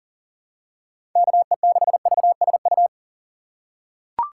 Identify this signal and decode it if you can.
Morse code, heard as a keyed tone, about 40 wpm, 700 Hz. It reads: KE6VSU
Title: USA Call Signs - 6 Characters - 3x Character Spacing - 40wpm